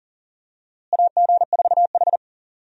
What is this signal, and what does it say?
Morse code, heard as a keyed tone, about 40 wpm, 700 Hz. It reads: AG4H
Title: USA Call Signs - 4 Characters - Repeated 2x - 40wpm